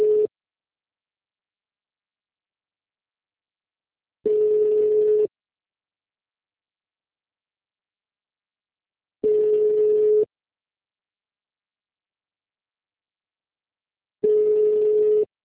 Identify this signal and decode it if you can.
Ringback tone (Europe, 425 Hz).
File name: out- -2058- - - .7557